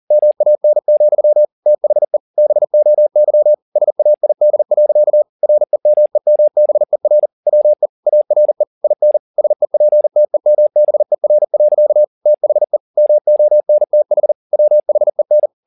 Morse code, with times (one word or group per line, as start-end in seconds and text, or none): 0.1-1.4 MAN,
1.7-2.2 THE
2.4-3.5 BOY
3.8-5.2 SAID.
5.4-7.3 REMEMBER
7.5-7.9 WE
8.1-8.6 ARE
8.8-9.2 IN
9.4-12.0 SEPTEMBER.
12.3-12.8 THE
13.0-14.3 MONTH
14.5-15.5 WHEN